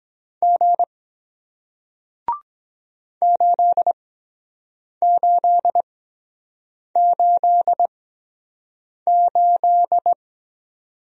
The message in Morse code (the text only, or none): G 8888